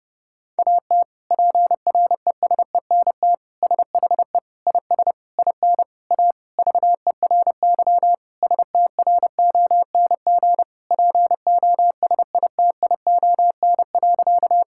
Morse code, heard as a keyed tone, about 30 wpm, 700 Hz. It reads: AT PRESENT SHE IS IN A VERY STRONG POSITION.
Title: Sentences from Top 1000 Words - Review Part 5 - 30wpm